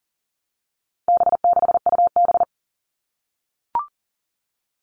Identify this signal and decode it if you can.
Morse code, heard as a keyed tone, about 40 wpm, 700 Hz. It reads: B6UB
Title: All Cut Numbers - 4-Digit 40wpm